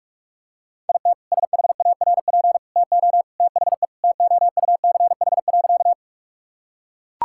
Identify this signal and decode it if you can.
Morse code, heard as a keyed tone, about 41 wpm, 700 Hz. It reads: IT SHARP TO THE TOUCH.
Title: Sentences from Top 1600 Words 45wpm